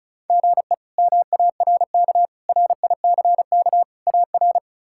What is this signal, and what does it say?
Morse code, heard as a keyed tone, about 34 wpm, 700 Hz. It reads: GE MARK RICK AR
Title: K1USN SST QSO Practice - 35wpm